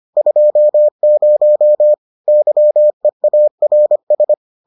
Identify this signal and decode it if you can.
Morse code, heard as a keyed tone, about 25 wpm, 600 Hz. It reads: 20 YEARS